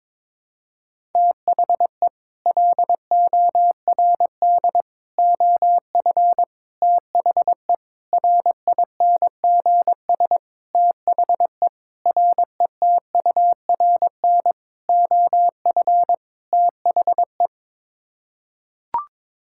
Morse code, heard as a keyed tone, about 22 wpm, 700 Hz. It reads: THE LORD OF THE RINGS THE RETURN OF THE
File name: Sets of 9 Words - 22wpm